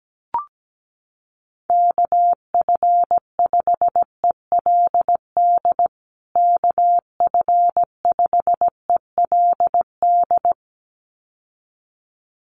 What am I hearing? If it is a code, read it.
Morse code, heard as a keyed tone, about 17 wpm, 700 Hz. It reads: KF5ELD KF5ELD
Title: USA Call Signs Repeated 2x 17wpm